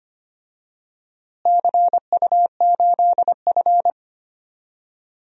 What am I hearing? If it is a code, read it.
Morse code, heard as a keyed tone, about 25 wpm, 700 Hz. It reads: CU8F
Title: DXCC-100 Call Signs - 4 Characters - Repeated 2x - 25wpm